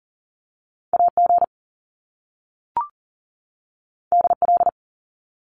Morse code, heard as a keyed tone, about 38 wpm, 700 Hz. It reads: AG DL
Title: Top Two Letter Combinations in Words 40wpm